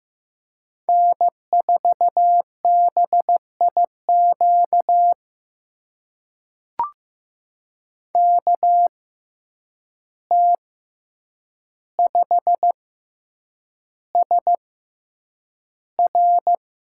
Morse code, heard as a keyed tone, about 15 wpm, spaced as for 8 wpm, 700 Hz. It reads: N4BIQ KT5SR
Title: USA Call Signs - 5 Characters - 6x Character Spacing - 15wpm